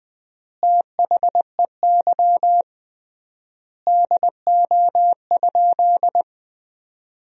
Morse code, heard as a keyed tone, about 20 wpm, 700 Hz. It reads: THEY DO?
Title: 3x Word-Spacing - Sentences from Top 100 Words - 20wpm